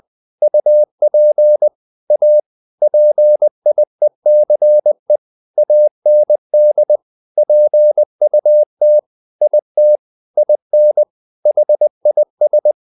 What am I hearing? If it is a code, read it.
Morse code, heard as a keyed tone, about 20 wpm, 600 Hz. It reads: UP A PIECE AND PUT IT IN HIS
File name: the old man and the sea / the old man and the sea part 1 20wpm